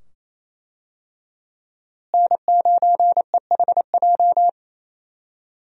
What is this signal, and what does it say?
Morse code, heard as a keyed tone, about 28 wpm, 700 Hz. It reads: N9EHJ
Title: USA Call Signs - 5 Characters - Repeated 2x - 28wpm